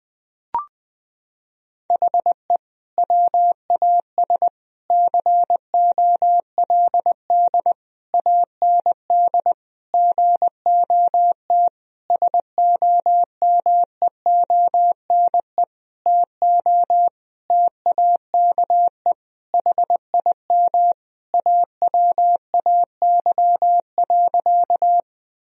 Morse code, heard as a keyed tone, about 20 wpm, 700 Hz. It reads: HE WAS COLD AND GOT SOMEONE TO TAKE HIM AWAY.
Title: Sentences from Top 1000 Words - Review Part 4 - 20wpm